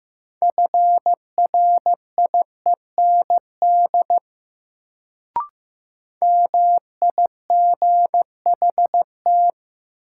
Morse code, heard as a keyed tone, about 15 wpm, 700 Hz. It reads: FRIEND MIGHT
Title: Sentences from Top 300 Words - New Word Drill - 15wpm